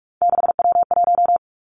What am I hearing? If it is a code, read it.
Morse code, heard as a keyed tone, about 46 wpm, 700 Hz. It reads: 6W1
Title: Three Challenging Letter-Number Combinations 45wpm